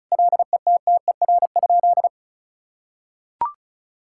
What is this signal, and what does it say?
Morse code, heard as a keyed tone, about 34 wpm, 700 Hz. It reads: LETTER?
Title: Sentences from Top 1000 Words - Review Part 6 - 35wpm